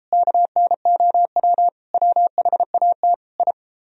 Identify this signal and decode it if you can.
Morse code, heard as a keyed tone, about 32 wpm, 700 Hz. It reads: KNOW WHAT I